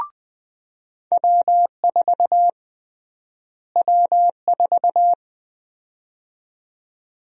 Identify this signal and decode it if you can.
Morse code, heard as a keyed tone, about 20 wpm, 700 Hz. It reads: W4 W4
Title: USA Call Sign Prefixes - Repeated 2x - 20wpm